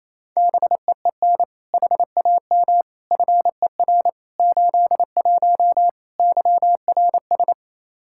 Morse code, heard as a keyed tone, about 28 wpm, 700 Hz. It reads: BEEN HAM FER 81 YRS
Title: Practice QSO - Exchange 3 - More Info - 28wpm